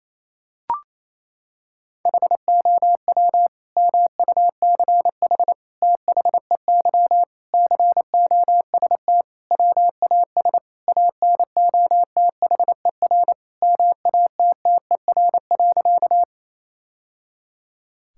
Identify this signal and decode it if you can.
Morse code, heard as a keyed tone, about 28 wpm, 700 Hz. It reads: HOW MUCH THEY COST WAS ANOTHER MATTER.
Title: Sentences from Top 1000 Words - Review Part 5 - 28wpm